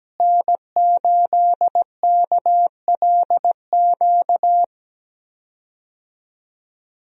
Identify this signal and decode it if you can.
Morse code, heard as a keyed tone, about 17 wpm, 700 Hz. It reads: N8KLQ
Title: USA Call Signs - 5 Characters - Repeated 3x - 17wpm